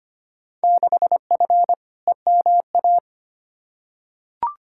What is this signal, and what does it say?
Morse code, heard as a keyed tone, about 25 wpm, 700 Hz. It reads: 6F EMA
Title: Field Day Exchanges Encore - 25wpm